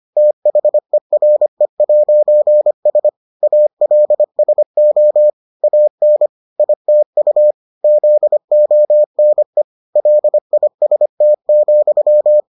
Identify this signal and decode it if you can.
Morse code, heard as a keyed tone, about 25 wpm, 600 Hz. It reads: THERE'S ALSO AN ITU ZONE LIST,